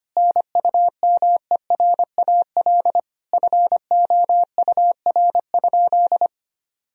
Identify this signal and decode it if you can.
Morse code, heard as a keyed tone, about 25 wpm, 700 Hz. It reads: NUMERAL FOUR?
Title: Sentences from Top 1000 Words - Review Part 2 - 25wpm